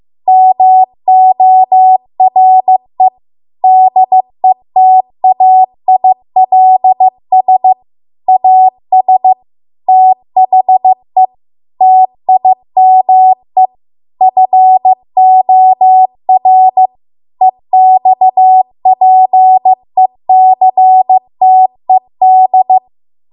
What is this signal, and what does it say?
Morse code, heard as a keyed tone, about 15 wpm, 750 Hz. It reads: MORE DETAILS AS THE TIME FOR EXPECTED